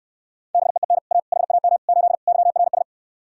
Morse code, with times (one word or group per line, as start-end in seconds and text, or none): 0.5-1.0 BEEN
1.1-1.2 A
1.3-1.8 HAM
1.9-2.2 8
2.3-2.8 YRS